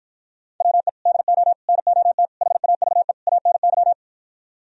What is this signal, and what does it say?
Morse code, heard as a keyed tone, about 52 wpm, 700 Hz. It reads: WE DO NOT HAVE ANY